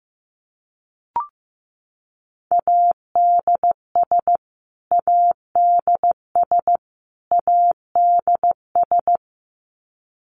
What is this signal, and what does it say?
Morse code, heard as a keyed tone, about 15 wpm, 700 Hz. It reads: ADS ADS ADS
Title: Abbreviations Repeated 3x - 15wpm